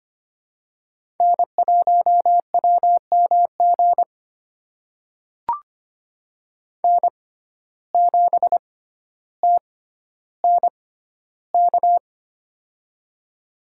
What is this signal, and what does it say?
Morse code, heard as a keyed tone, about 25 wpm, spaced as for 13 wpm, 700 Hz. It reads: N1WMG N7TNK
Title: USA Call Signs - 5 Characters - 6x Character Spacing - 25wpm